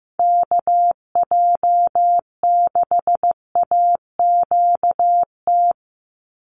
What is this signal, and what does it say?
Morse code, heard as a keyed tone, about 15 wpm, 700 Hz. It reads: KJ6AQT
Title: USA Call Signs 15wpm